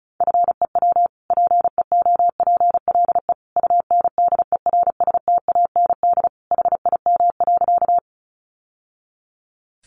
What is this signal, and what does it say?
Morse code, heard as a keyed tone, about 34 wpm, 700 Hz. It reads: FEW PEOPLE UNDERSTAND HIM.
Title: Sentences from Top 600 Words 35wpm